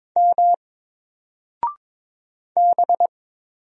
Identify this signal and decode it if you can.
Morse code, heard as a keyed tone, about 23 wpm, 700 Hz. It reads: M B
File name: Single-Letter-Number-22wpm